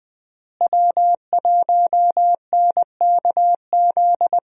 Morse code, heard as a keyed tone, about 20 wpm, 700 Hz. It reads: W1NKZ
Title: USA Call Signs - 5 Characters - Repeated 3x - 20wpm